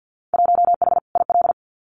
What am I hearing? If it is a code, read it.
Morse code, heard as a keyed tone, about 50 wpm, 700 Hz. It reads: 1H IL
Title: Winter Field Day Exchanges 50wpm